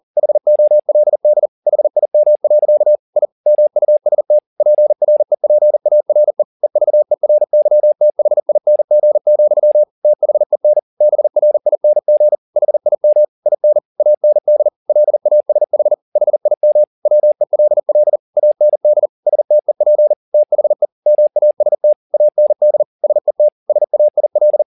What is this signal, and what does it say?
Morse code, heard as a keyed tone, about 40 wpm, 600 Hz. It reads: HOLD HIM. I MUST PREPARE EVERYTHING, THEN BRING HIM IN AND LASH HIM WELL AND STEP THE MAST AND SET SAIL
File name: the old man and the sea 2 40wpm